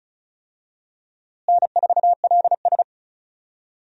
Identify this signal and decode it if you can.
Morse code, heard as a keyed tone, about 36 wpm, 700 Hz. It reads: N4LS
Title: USA Call Signs - 4 Characters - Repeated 3x - 35wpm